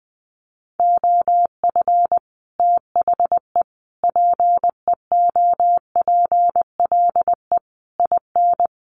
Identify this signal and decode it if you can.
Morse code, heard as a keyed tone, about 20 wpm, 700 Hz. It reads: OF THE PEOPLE IN
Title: Sets of 5 Words Encore 20wpm